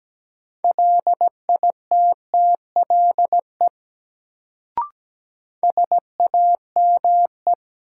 Morse code, heard as a keyed tone, about 17 wpm, 700 Hz. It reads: LITTLE SAME
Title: Sentences from Top 200 Words - New Word Drill - 17wpm